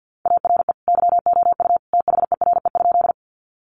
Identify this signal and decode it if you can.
Morse code, heard as a keyed tone, about 50 wpm, 700 Hz. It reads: ARE YOU THERE?
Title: Sentences from Top 100 Words 50wpm